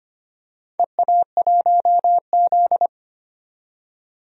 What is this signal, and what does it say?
Morse code, heard as a keyed tone, about 25 wpm, 700 Hz. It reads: EA1Z
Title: DXCC-100 Call Signs - 4 Characters - 25wpm